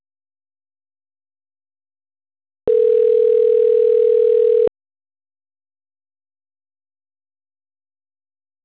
Ringback tone (North America), 440 + 480 Hz.